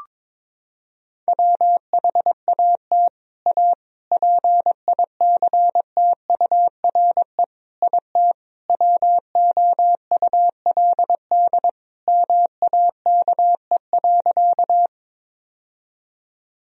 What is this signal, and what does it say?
Morse code, heard as a keyed tone, about 21 wpm, 700 Hz. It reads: WHAT A PICTURE IT WOULD MAKE.
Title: Sentences from Top 1000 Words - Review Part 1 - 22wpm